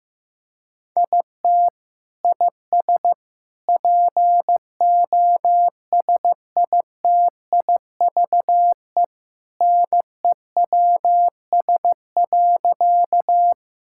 Morse code, heard as a keyed tone, about 15 wpm, 700 Hz. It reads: IT IS POSITIVE NEWS.
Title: Sentences from Top 1000 Words - Review Part 5 - 15wpm